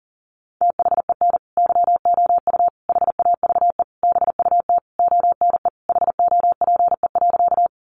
Morse code, heard as a keyed tone, about 37 wpm, 700 Hz. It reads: THEN YOU HAVE BUT ONE HOPE.
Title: Sentences from Top 1000 Words 40wpm